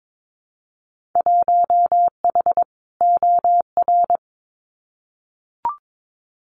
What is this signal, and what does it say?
Morse code, heard as a keyed tone, about 22 wpm, 700 Hz. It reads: 1H OR
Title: Winter Field Day Exchanges 22wpm